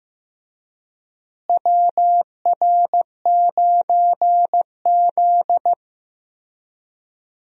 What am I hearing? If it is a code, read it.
Morse code, heard as a keyed tone, about 15 wpm, 700 Hz. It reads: WR9Z
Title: USA Call Signs - 4 Characters - Repeated 2x - 15wpm